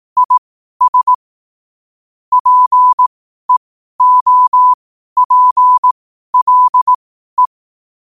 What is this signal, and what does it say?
Morse code, heard as a keyed tone, about 18 wpm, spaced as for 12 wpm, 1000 Hz. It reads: IS PEOPLE
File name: qotdfiles / 12wpm